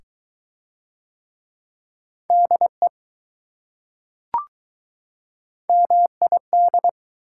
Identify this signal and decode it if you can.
Morse code, heard as a keyed tone, about 23 wpm, 700 Hz. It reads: DE MID